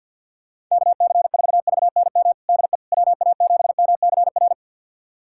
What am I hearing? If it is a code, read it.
Morse code, heard as a keyed tone, about 46 wpm, 700 Hz. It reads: KC4VNM DE WA7MCR
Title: USA Callsign DE Callsign - 6 Characters - 50wpm